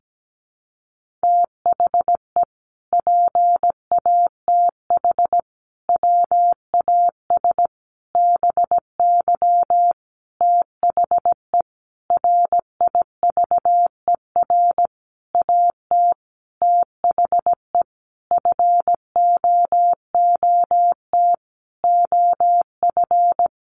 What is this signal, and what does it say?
Morse code, heard as a keyed tone, about 17 wpm, 700 Hz. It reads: THE PATH WAS BY THE RIVER AT THE FOOT OF
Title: Sentences from Top 1400 Words 17wpm